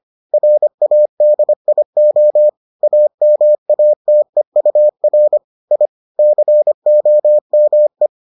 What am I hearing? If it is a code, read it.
Morse code, heard as a keyed tone, about 25 wpm, 600 Hz. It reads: RADIO AMATEUR I COME